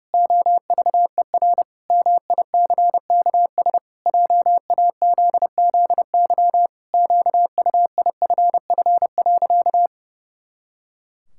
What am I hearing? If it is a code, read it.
Morse code, heard as a keyed tone, about 30 wpm, 700 Hz. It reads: OVER MICKS JAZZY QUIFF.
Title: Pangram Challenge v2 - 30wpm